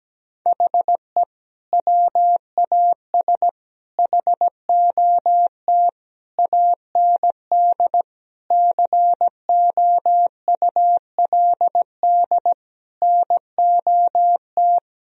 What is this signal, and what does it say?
Morse code, heard as a keyed tone, about 17 wpm, 700 Hz. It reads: HE WAS HOT AND COULD NOT
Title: Sentences from Top 1000 Words - Review Part 3 - 17wpm